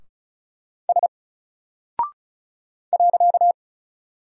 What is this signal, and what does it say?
Morse code, heard as a keyed tone, about 34 wpm, 700 Hz. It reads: S .
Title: Single Character (Letter:Number:Punctuation:Prosign) 35wpm